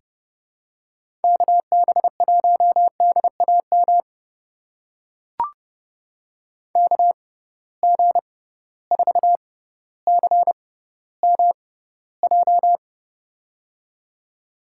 Morse code, heard as a keyed tone, about 30 wpm, spaced as for 15 wpm, 700 Hz. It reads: KB1DAM KG4CMJ
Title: USA Call Signs - 6 Characters - 6x Character Spacing - 30wpm